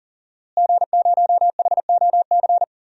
Morse code, heard as a keyed tone, about 40 wpm, 700 Hz. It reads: G0HOC
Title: DXCC-100 Call Signs - 5 Characters - Repeated 3x - 40wpm